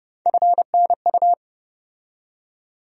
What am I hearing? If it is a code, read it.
Morse code, heard as a keyed tone, about 32 wpm, 700 Hz. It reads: FNU
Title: USA Call Sign Postfixes - 3 Letters 30wpm